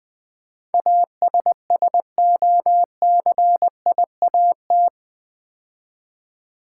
Morse code, heard as a keyed tone, about 20 wpm, 700 Hz. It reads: ASSOCIAT
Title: Build A Word - 4 Letters At A Time - 11-Letter Words - 20wpm